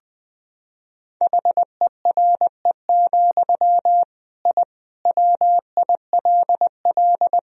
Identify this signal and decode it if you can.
Morse code, heard as a keyed tone, about 21 wpm, 700 Hz. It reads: HERE, I WILL
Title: Sentences from Top 1000 Words - Review Part 1 - 20wpm